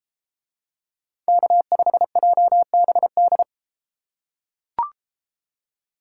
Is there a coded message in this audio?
Yes: Morse code from 1.3 to 3.4 s, heard as a keyed tone, about 31 wpm, 700 Hz. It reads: K5JBD